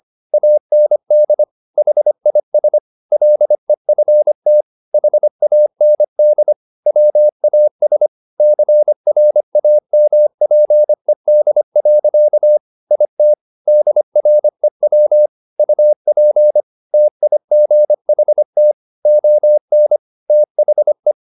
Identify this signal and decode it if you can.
Morse code, heard as a keyed tone, about 25 wpm, 600 Hz. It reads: AND HIS LEFT HAND WAS CRAMPED. IT DREW UP TIGHT ON THE
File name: the old man and the sea part 1 25wpm